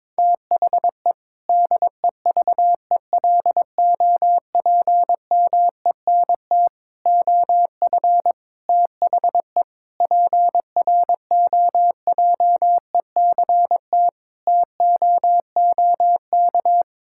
Morse code, heard as a keyed tone, about 22 wpm, 700 Hz. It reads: THE DEVELOPMENT OF THE PROJECT TOOK
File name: practice / 22wpm / Sentences with 3-Word Phrases - 22wpm